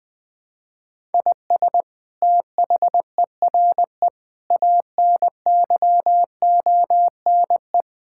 Morse code, heard as a keyed tone, about 21 wpm, 700 Hz. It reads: IS THERE ANYONE
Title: Sentences from Top 1000 Words - Review Part 6 - 20wpm